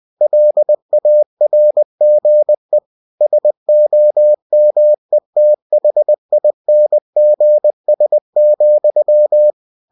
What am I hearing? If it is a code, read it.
Morse code, heard as a keyed tone, about 20 wpm, 600 Hz. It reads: LARGE SOMETHINGS,
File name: cw-books / winnie-the-pooh / winnie-20wpm-page-0030